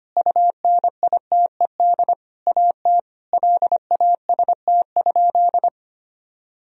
Morse code, heard as a keyed tone, about 25 wpm, 700 Hz. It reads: UNITED AT LAST?
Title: Sentences from Top 1000 Words - Review Part 4 - 25wpm